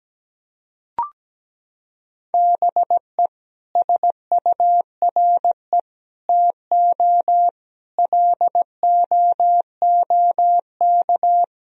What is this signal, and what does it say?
Morse code, heard as a keyed tone, about 17 wpm, 700 Hz. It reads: BE SURE TO LOOK
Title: Sentences from Top 1000 Words - Review Part 4 - 17wpm